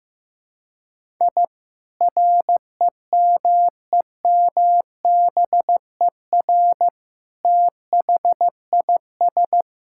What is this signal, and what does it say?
Morse code, heard as a keyed tone, about 16 wpm, 700 Hz. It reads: I REMEMBER THIS
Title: Sentences from Top 1000 Words - Review Part 5 - 15wpm